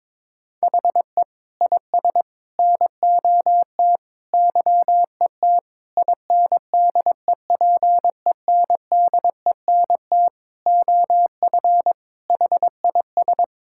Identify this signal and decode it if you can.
Morse code, heard as a keyed tone, about 22 wpm, 700 Hz. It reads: HE IS NOT YET INDEPENDENT OF HIS
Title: Sentences from Top 1000 Words - Review Part 6 - 22wpm